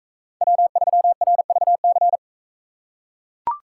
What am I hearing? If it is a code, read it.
Morse code, heard as a keyed tone, about 38 wpm, 700 Hz. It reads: W3RVC